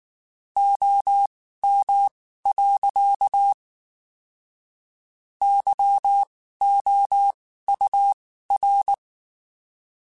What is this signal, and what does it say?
Morse code, heard as a keyed tone, about 19 wpm, spaced as for 14 wpm, 800 Hz. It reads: OM. YOUR
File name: (spaced) / QSOs Spaced-mp3s / kc7haa @18 sp